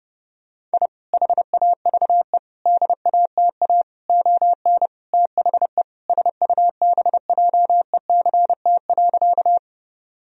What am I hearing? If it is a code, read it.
Morse code, heard as a keyed tone, about 30 wpm, 700 Hz. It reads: I HAVE DATA ON THE SUBJECT.
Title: Sentences from Top 1000 Words - Review Part 3 - 30wpm